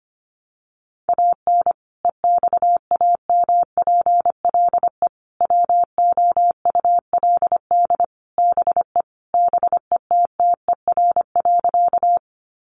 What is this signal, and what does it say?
Morse code, heard as a keyed tone, about 25 wpm, 700 Hz. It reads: AN EXAMPLE WOULD BE BETTER.
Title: Sentences from Top 400 Words 25wpm